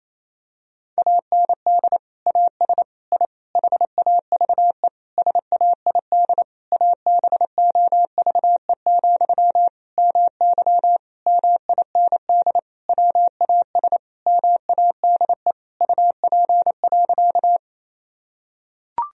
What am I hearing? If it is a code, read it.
Morse code, heard as a keyed tone, about 28 wpm, 700 Hz. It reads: AND AS I HAVE SAID ABOVE, MY MIND WAS MADE UP.
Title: Sentences from Top 1000 Words - Review Part 2 - 28wpm